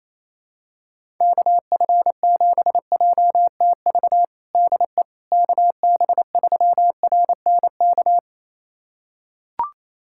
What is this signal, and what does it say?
Morse code, heard as a keyed tone, about 28 wpm, 700 Hz. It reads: KF7JTV DE KB3RNK
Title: USA Callsign DE Callsign 28wpm